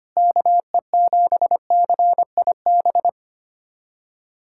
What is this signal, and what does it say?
Morse code, heard as a keyed tone, about 25 wpm, 700 Hz. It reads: KE7CIB
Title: USA Callsign DE Callsign - 6 Characters - 25wpm